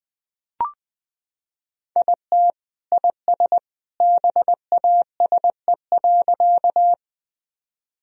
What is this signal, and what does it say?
Morse code, heard as a keyed tone, about 20 wpm, 700 Hz. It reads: IT IS BASE.
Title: Sentences from Top 1000 Words - Review Part 1 - 20wpm